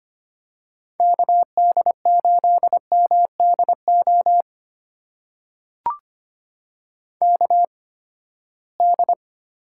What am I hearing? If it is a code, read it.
Morse code, heard as a keyed tone, about 25 wpm, spaced as for 17 wpm, 700 Hz. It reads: KD8MDO KD
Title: USA Call Signs - 6 Characters - 8x Character Spacing - 25wpm